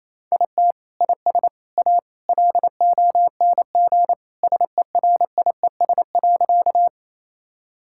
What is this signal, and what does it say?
Morse code, heard as a keyed tone, about 28 wpm, 700 Hz. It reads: IT IS A LONG SERIES.
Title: Sentences from Top 1000 Words - Review Part 5 - 28wpm